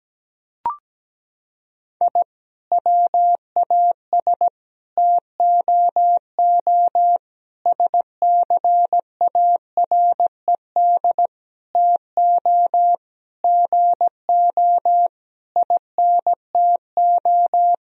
Morse code, heard as a keyed tone, about 17 wpm, 700 Hz. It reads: I WAS TOO SCARED TO GO INTO
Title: Sentences from Top 1000 Words - Review Part 4 - 17wpm